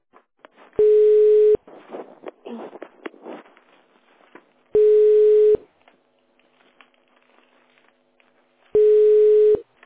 Ringback tone (425 Hz, 1 s on, 3 s off).